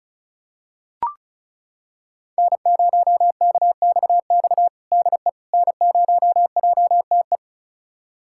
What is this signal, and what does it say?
Morse code, heard as a keyed tone, about 36 wpm, 700 Hz. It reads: N0KXX DE N0JTE
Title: USA Callsign DE Callsign - 5 Characters - 35wpm